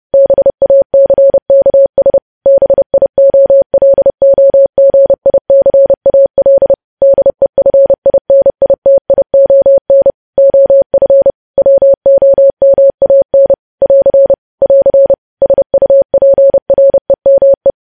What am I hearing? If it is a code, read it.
Morse code, heard as a keyed tone, about 30 wpm, 550 Hz. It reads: BACKS BIOLOGICAL DEFINITION OF WOMAN <AR> <AR> SUPREME
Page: news headlines in morse code at 30 wpm